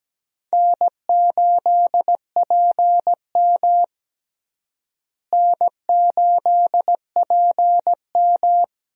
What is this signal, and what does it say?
Morse code, heard as a keyed tone, about 17 wpm, 700 Hz. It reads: N8PM N8PM
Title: USA Call Signs - 4 Characters - Repeated 2x - 17wpm